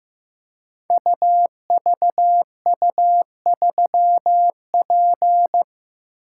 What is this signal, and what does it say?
Morse code, heard as a keyed tone, about 15 wpm, 700 Hz. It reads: UVU3P
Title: Code Groups - 5 Letter:Numbers - 15wpm